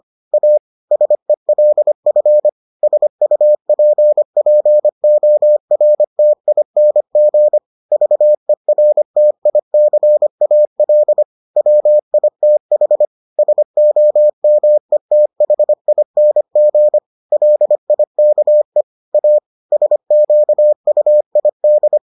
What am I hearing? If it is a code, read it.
Morse code, heard as a keyed tone, about 25 wpm, 600 Hz. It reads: A SELF SUPPORTING VERTICAL WITH SOMETHING LIKE A SQUID